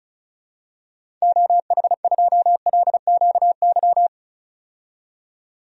Morse code, heard as a keyed tone, about 34 wpm, 700 Hz. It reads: OH2LQY
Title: DXCC-100 Call Signs - 6 Characters - 35wpm